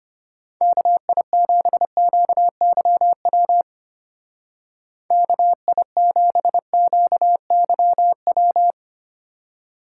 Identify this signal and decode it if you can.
Morse code, heard as a keyed tone, about 32 wpm, 700 Hz. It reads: KI7QYW KI7QYW